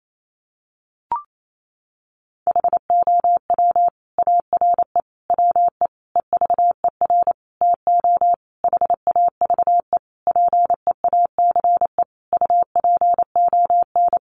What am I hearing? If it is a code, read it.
Morse code, heard as a keyed tone, about 28 wpm, 700 Hz. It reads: HOW ARE WE EVER TO HAVE PEACE UPON
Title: Sentences from Top 1500 Words 28wpm